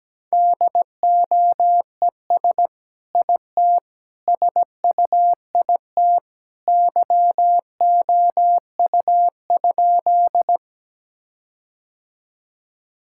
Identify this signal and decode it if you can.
Morse code, heard as a keyed tone, about 17 wpm, 700 Hz. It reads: DOES IT SUIT YOU?
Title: Sentences from Top 1600 Words 17wpm